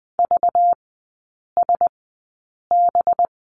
Morse code, heard as a keyed tone, about 20 wpm, spaced as for 8 wpm, 700 Hz. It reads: VSB
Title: 191 Lesson 31e - Letter v - Three Characters 20wpm